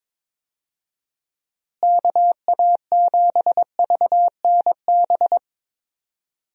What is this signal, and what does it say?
Morse code, heard as a keyed tone, about 22 wpm, 700 Hz. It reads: KA7VNB
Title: USA Call Signs - 6 Characters - Repeated 3x - 22wpm